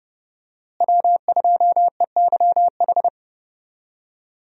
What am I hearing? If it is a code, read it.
Morse code, heard as a keyed tone, about 30 wpm, 700 Hz. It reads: W2EYH
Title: USA Call Signs - 5 Characters - Repeated 2x - 30wpm